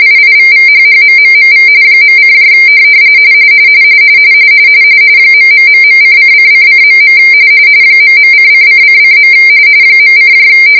RTTY, about 45.45 baud. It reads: IIVVHHMMMV .: ....:::::.. ..:::::. .: